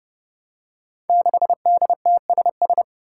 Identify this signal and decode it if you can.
Morse code, heard as a keyed tone, about 30 wpm, 700 Hz. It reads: 6DTSS
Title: Five Challenging Letter-Number Combinations 30wpm